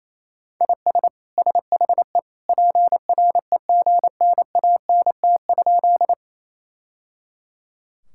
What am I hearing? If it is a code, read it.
Morse code, heard as a keyed tone, about 28 wpm, 700 Hz. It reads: IS SHE PREGNANT?
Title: Sentences from Top 1000 Words - Review Part 6 - 28wpm